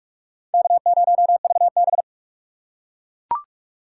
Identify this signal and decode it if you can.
Morse code, heard as a keyed tone, about 42 wpm, 700 Hz. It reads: K0VB
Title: USA Call Signs - 4 Characters - 3x Character Spacing - 45wpm